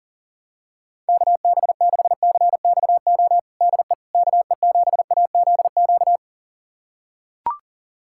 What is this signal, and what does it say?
Morse code, heard as a keyed tone, about 40 wpm, 700 Hz. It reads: KB6CXO DE KE7AZQ
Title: USA Callsign DE Callsign - 6 Characters - 40wpm